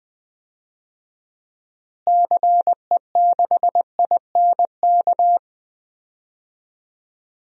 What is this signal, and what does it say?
Morse code, heard as a keyed tone, about 20 wpm, 700 Hz. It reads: CE6INK
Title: DXCC-100 Call Signs - 6 Characters - Repeated 2x - 20wpm